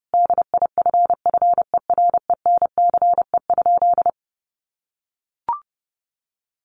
Morse code, heard as a keyed tone, about 30 wpm, 700 Hz. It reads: DIFFERENCE?
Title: Sentences from Top 800 Words 30wpm